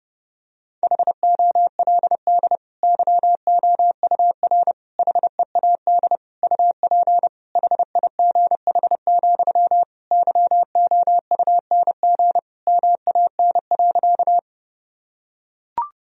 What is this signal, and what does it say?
Morse code, heard as a keyed tone, about 30 wpm, 700 Hz. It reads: HOLD YOUR HEAD UP HIGH, YOUNG MAN.
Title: Sentences from Top 1000 Words - Review Part 2 - 30wpm